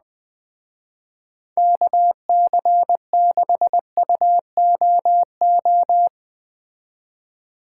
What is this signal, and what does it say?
Morse code, heard as a keyed tone, about 19 wpm, 700 Hz. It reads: KC6UOO
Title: Speed Racing - USA 2x3 Call Signs 17wpm